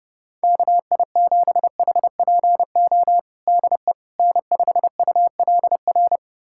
Morse code, heard as a keyed tone, about 30 wpm, 700 Hz. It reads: KI7HPO DE N5ULR
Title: USA Callsign DE Callsign 30wpm